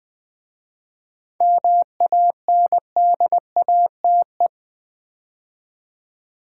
Morse code, heard as a keyed tone, about 20 wpm, 700 Hz. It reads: MANDATE
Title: 7 Letter Words - First 3 Letters Given - 20wpm